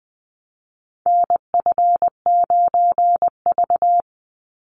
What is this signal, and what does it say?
Morse code, heard as a keyed tone, about 20 wpm, 700 Hz. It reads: NF9V
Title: Call Signs Repeated 2x - 20wpm